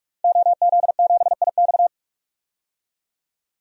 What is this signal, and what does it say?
Morse code, heard as a keyed tone, about 46 wpm, 700 Hz. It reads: OZ7IX
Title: DXCC-100 Call Signs - 5 Characters - Repeated 3x - 45wpm